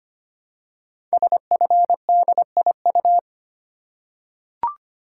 Morse code, heard as a keyed tone, about 25 wpm, 700 Hz. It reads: SFDIU
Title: Code Groups - 5 Letters - 25wpm